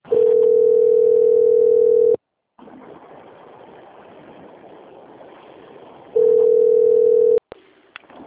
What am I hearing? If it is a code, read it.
Ringback tone (North America), 440 + 480 Hz.